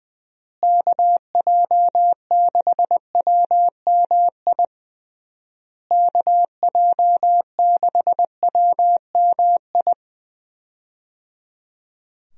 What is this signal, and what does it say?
Morse code, heard as a keyed tone, about 20 wpm, 700 Hz. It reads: KJ6WMI KJ6WMI
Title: USA Call Signs - 6 Characters - Repeated 3x - 20wpm